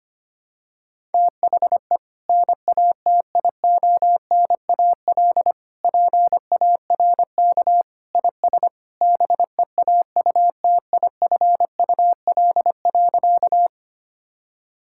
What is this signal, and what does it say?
Morse code, heard as a keyed tone, about 25 wpm, 700 Hz. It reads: THE NATIONAL PARK IS BEAUTIFUL.
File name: practice / 25wpm / Sentences with 2-Word Phrases - 25wpm